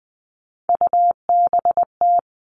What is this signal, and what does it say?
Morse code, heard as a keyed tone, about 20 wpm, 700 Hz. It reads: UBT
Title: All Cut Numbers - 3-Digit 20wpm